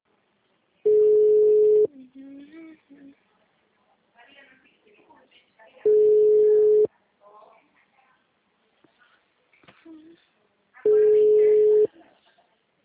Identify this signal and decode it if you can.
Ringback tone (Europe, 425 Hz), with speech.